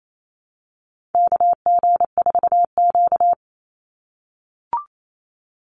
Morse code, heard as a keyed tone, about 28 wpm, 700 Hz. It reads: KG4Q
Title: Call Signs - 28wpm